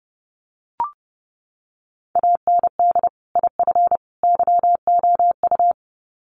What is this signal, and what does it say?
Morse code, heard as a keyed tone, about 30 wpm, 700 Hz. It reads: AND IF YOU
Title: Sets of 3 Words 30wpm